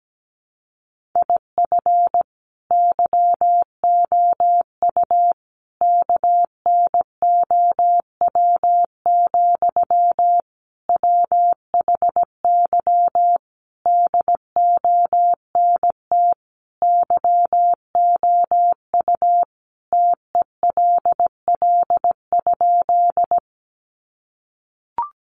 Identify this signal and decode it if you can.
Morse code, heard as a keyed tone, about 17 wpm, 700 Hz. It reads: IF YOU KNOW, WHY DONT YOU TELL?
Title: Sentences from Top 300 Words 17wpm